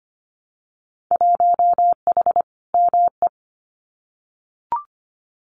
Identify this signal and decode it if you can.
Morse code, heard as a keyed tone, about 25 wpm, 700 Hz. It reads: 1H ME
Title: Winter Field Day Exchanges 25wpm